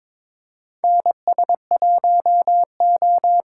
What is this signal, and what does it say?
Morse code, heard as a keyed tone, about 21 wpm, 700 Hz. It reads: NS1O
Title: USA Callsign DE Callsign - 4 Characters - 22wpm